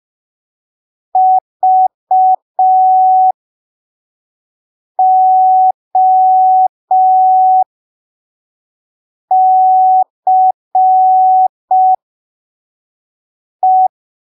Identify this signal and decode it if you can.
Morse code, heard as a keyed tone, about 5 wpm, spaced as for 3 wpm, 750 Hz. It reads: VOCE